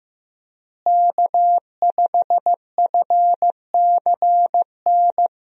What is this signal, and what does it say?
Morse code, heard as a keyed tone, about 15 wpm, 700 Hz. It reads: K5FCN
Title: USA Call Signs - 5 Characters - Repeated 3x - 15wpm